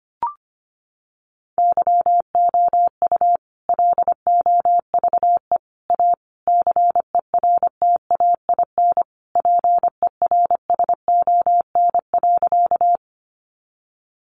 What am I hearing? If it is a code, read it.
Morse code, heard as a keyed tone, about 25 wpm, 700 Hz. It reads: YOU LOVE A CERTAIN PERSON.
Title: Sentences from Top 500 Words 25wpm